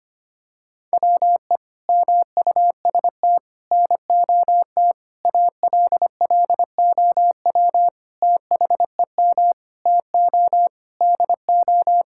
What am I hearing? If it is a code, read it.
Morse code, heard as a keyed tone, about 25 wpm, 700 Hz. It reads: WE MUST NOT ALLOW THEM TO DO
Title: Sentences from Top 1000 Words - Review Part 5 - 25wpm